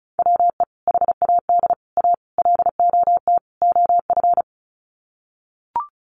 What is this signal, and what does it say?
Morse code, heard as a keyed tone, about 34 wpm, 700 Hz. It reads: WE HAD A LOT OF
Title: Sets of 5 Words Encore 35wpm